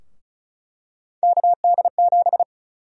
Morse code, heard as a keyed tone, about 36 wpm, 700 Hz. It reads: KD7
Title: USA Call Sign Prefixes - Repeated 2x - 35wpm